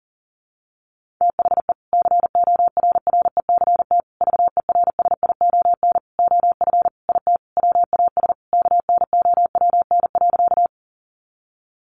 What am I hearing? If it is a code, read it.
Morse code, heard as a keyed tone, about 37 wpm, 700 Hz. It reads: THE CORRECT VERSION OF IT WAS KNOWN.
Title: Sentences from Top 1400 Words 40wpm